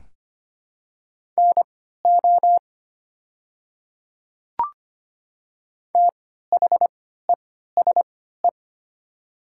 Morse code, heard as a keyed tone, about 25 wpm, spaced as for 14 wpm, 700 Hz. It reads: NO THESE
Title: 3x Character Spacing - Top 500 Words - 25wpm